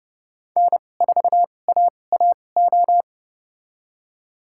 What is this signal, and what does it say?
Morse code, heard as a keyed tone, about 30 wpm, spaced as for 22 wpm, 690 Hz. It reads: N4AAO
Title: USA Call Signs - 5 Characters - 2x Character Spacing - 30wpm